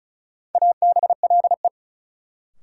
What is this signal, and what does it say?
Morse code, heard as a keyed tone, about 36 wpm, 700 Hz. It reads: ABLE
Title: Sentences from Top 700 Words - New Word Drill - 35wpm